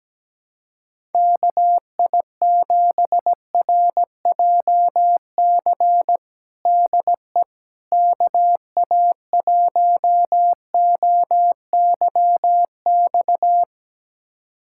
Morse code, heard as a keyed tone, about 17 wpm, 700 Hz. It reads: KI7RJC DE KA1OYX
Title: USA Callsign DE Callsign - 6 Characters - 17wpm